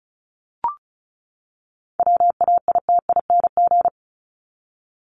Morse code, heard as a keyed tone, about 34 wpm, 700 Hz. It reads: WAITING
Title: ING Suffix - Top 100 Words with Base Words 35wpm